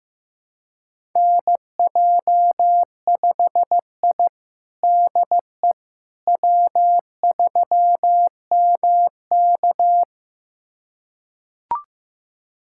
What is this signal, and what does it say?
Morse code, heard as a keyed tone, about 15 wpm, 700 Hz. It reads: NJ5I DE W3MK
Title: USA Callsign DE Callsign - 4 Characters - 15wpm